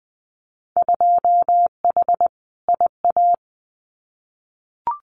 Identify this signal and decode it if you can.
Morse code, heard as a keyed tone, about 20 wpm, 700 Hz. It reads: 2H IA
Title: Winter Field Day Exchanges Encore 20wpm